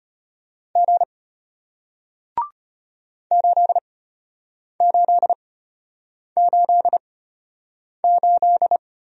Morse code, heard as a keyed tone, about 32 wpm, spaced as for 8 wpm, 700 Hz. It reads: G 8888